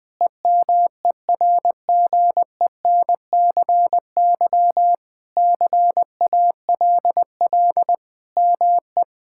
Morse code, heard as a keyed tone, about 20 wpm, 700 Hz. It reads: EMERGENCY CALL ME
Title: Sentences with 3-Word Phrases - 20wpm